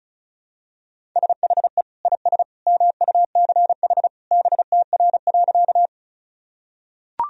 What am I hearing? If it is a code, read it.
Morse code, heard as a keyed tone, about 36 wpm, 700 Hz. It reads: SHE IS MUCH BTR.
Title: Sentences from Top 400 Words and Abbreviations 35wpm